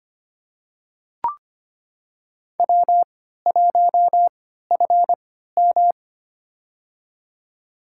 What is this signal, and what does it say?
Morse code, heard as a keyed tone, about 25 wpm, spaced as for 14 wpm, 700 Hz. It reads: W1FM
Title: USA Call Signs - 4 Characters - 3x Character Spacing - 25wpm